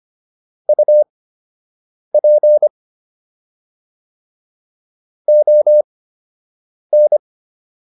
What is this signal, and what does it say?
Morse code, heard as a keyed tone, about 25 wpm, spaced as for 7 wpm, 600 Hz. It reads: UP ON